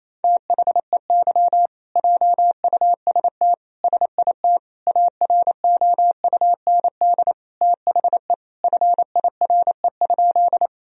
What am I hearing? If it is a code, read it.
Morse code, heard as a keyed tone, about 27 wpm, 700 Hz. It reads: THEY JUST SIT AROUND THE FIRE?
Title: Sentences from Top 1000 Words - Review Part 3 - 28wpm